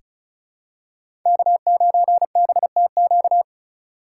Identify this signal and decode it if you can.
Morse code, heard as a keyed tone, about 36 wpm, 700 Hz. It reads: K9BTQ
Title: K1USN SST QSO Practice - 35wpm